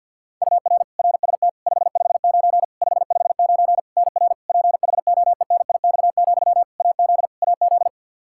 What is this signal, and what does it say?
Morse code, heard as a keyed tone, about 50 wpm, 700 Hz. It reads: UR RST 559 559 NR PHOENIX, AZ AZ
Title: Practice QSO - Exchange 1 - Beginning - 50wpm